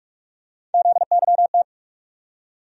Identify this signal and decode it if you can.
Morse code, heard as a keyed tone, about 45 wpm, 700 Hz. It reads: ZYT
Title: USA Call Sign Postfixes - 3 Letters 45wpm